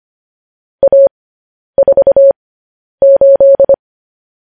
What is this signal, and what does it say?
Morse code, heard as a keyed tone, about 25 wpm, spaced as for 10 wpm, 550 Hz. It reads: A48
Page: news headlines in morse code at 10 wpm